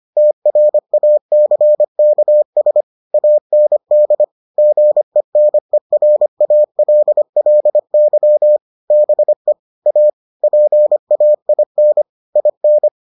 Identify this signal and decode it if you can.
Morse code, heard as a keyed tone, about 25 wpm, 600 Hz. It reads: TRACKS AND GENERALLY BE A PAIN IN